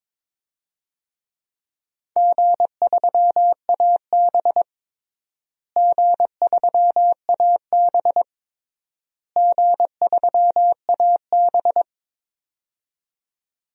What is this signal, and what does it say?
Morse code, heard as a keyed tone, about 22 wpm, 700 Hz. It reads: G3AB G3AB G3AB
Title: DXCC-100 Call Signs - 4 Characters - Repeated 3x - 22wpm